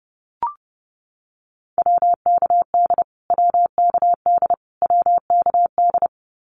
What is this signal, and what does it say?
Morse code, heard as a keyed tone, about 30 wpm, 700 Hz. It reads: WKD WKD WKD
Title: Abbreviations Repeated 3x - 30wpm